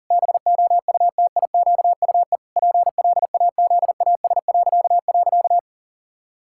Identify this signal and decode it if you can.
Morse code, heard as a keyed tone, about 40 wpm, 700 Hz. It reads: BOUTIQUE PLAZAS..
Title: Pangram Challenge v2 - 40wpm